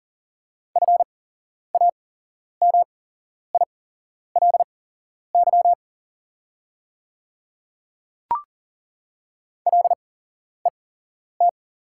Morse code, heard as a keyed tone, about 40 wpm, spaced as for 11 wpm, 690 Hz. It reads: FAMILY LET
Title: 8x Character Spacing - Top 500 Words - 40wpm